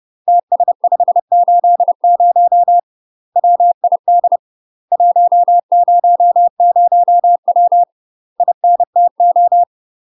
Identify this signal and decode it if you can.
Morse code, heard as a keyed tone, about 30 wpm, 700 Hz. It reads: TS580 WID 100W INTO
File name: UK MASTER QSOs TRAINING / 30WPM / UK MASTER QSOs 0051